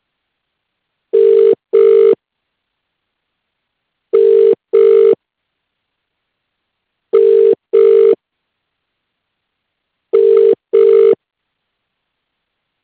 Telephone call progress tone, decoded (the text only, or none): ringback tone